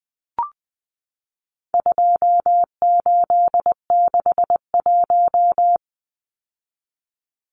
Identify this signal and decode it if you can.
Morse code, heard as a keyed tone, about 20 wpm, 700 Hz. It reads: 2861
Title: Numbers - 4-Digit 20wpm